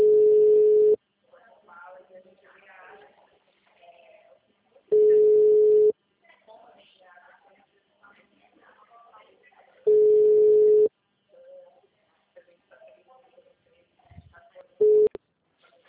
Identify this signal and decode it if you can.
Ringback tone (Europe, 425 Hz), with speech.